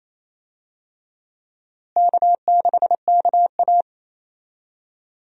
Morse code, heard as a keyed tone, about 27 wpm, 700 Hz. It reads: K6KA
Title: USA Call Signs - 4 Characters - Repeated 2x - 28wpm